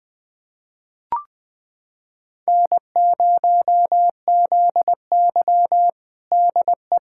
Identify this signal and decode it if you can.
Morse code, heard as a keyed tone, about 20 wpm, 700 Hz. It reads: N0ZY DE
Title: USA Callsign DE Callsign - 4 Characters - 20wpm